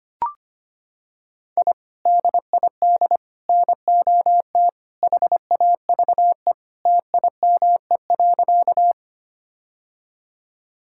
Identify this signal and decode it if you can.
Morse code, heard as a keyed tone, about 25 wpm, 700 Hz. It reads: I DID NOT HAVE TIME.
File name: practice / 25wpm / Sentences with 3-Word Phrases - 25wpm